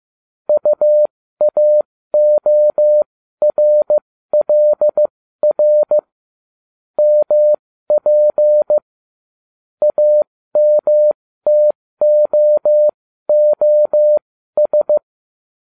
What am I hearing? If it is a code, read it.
Morse code, heard as a keyed tone, about 15 wpm, spaced as for 13 wpm, 600 Hz. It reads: UAORLR MP AMTOOS